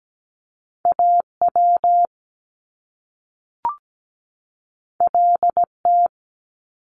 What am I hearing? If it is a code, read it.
Morse code, heard as a keyed tone, about 17 wpm, 700 Hz. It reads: AW LT
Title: Top Two Letter Combinations in Words 17wpm